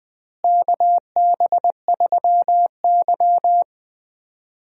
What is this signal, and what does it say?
Morse code, heard as a keyed tone, about 20 wpm, 700 Hz. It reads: KB3Y
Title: USA Call Signs - 4 Characters - Repeated 3x - 20wpm